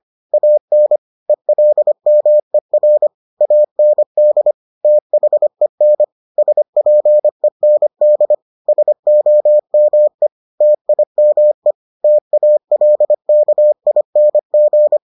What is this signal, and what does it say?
Morse code, heard as a keyed tone, about 25 wpm, 600 Hz. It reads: AN ELMER AND THEN SPEND SOME TIME TALKING